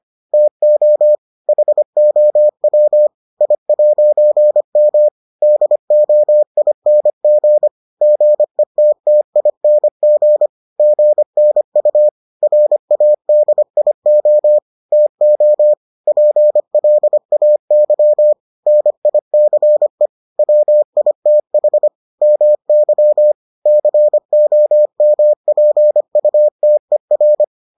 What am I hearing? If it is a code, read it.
Morse code, heard as a keyed tone, about 25 wpm, 600 Hz. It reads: TO HOW I'M DOING GETTING GNU RADIO TO PLAY NICE WITH MY COMPUTER